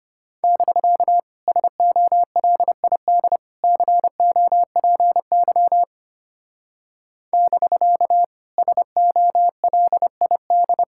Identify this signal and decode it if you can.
Morse code, heard as a keyed tone, about 30 wpm, 700 Hz. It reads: <BK> SOLID COPY <BK> SOLID
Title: Speed-Racing Common QSO Phrases - 25wpm